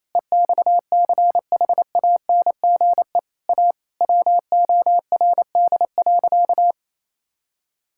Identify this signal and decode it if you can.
Morse code, heard as a keyed tone, about 28 wpm, 700 Hz. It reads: EXCHANGE A WORD.
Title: Sentences from Top 1000 Words - Review Part 6 - 28wpm